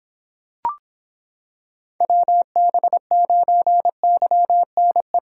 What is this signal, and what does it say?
Morse code, heard as a keyed tone, about 26 wpm, 700 Hz. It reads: WB9YNE